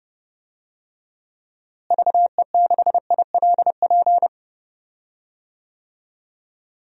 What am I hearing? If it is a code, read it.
Morse code, heard as a keyed tone, about 31 wpm, 700 Hz. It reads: VE6ILP
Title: DXCC-100 Call Signs - 6 Characters - Repeated 2x - 30wpm